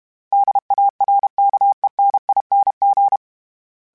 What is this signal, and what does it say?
Morse code, heard as a keyed tone, about 32 wpm, 800 Hz. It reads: DARKENING
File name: Special INGs-mp3s / Special INGs-mp3s / Sngl Ing @30